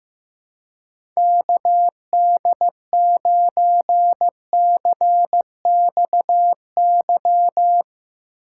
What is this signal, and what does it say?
Morse code, heard as a keyed tone, about 15 wpm, 700 Hz. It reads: KD9CXY
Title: USA Call Signs - 6 Characters - 2x Character Spacing - 15wpm